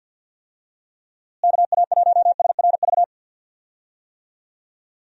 Morse code, heard as a keyed tone, about 51 wpm, 700 Hz. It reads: KA1SRV
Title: USA Call Signs - 6 Characters - Repeated 3x - 50wpm